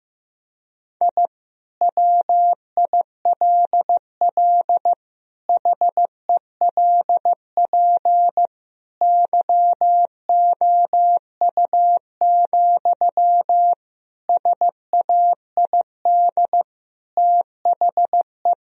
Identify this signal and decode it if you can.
Morse code, heard as a keyed tone, about 15 wpm, 700 Hz. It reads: I WILL HELP YOU, SAID THE
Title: Sentences from Top 1000 Words - Review Part 3 - 15wpm